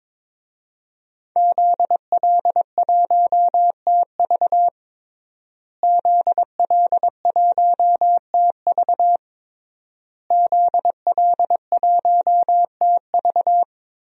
Morse code, heard as a keyed tone, about 22 wpm, 700 Hz. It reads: ZL1TV ZL1TV ZL1TV
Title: DXCC-100 Call Signs - 5 Characters - Repeated 3x - 22wpm